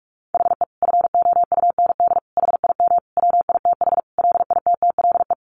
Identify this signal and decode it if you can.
Morse code, heard as a keyed tone, about 44 wpm, 700 Hz. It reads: HE FOUND HIM WITH LITTLE
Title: Sentences from Top 1500 Words 45wpm